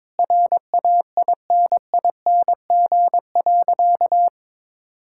Morse code, heard as a keyed tone, about 22 wpm, 700 Hz. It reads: RAINING.
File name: practice / 22wpm / Sentences with 4-Word Phrases - 22wpm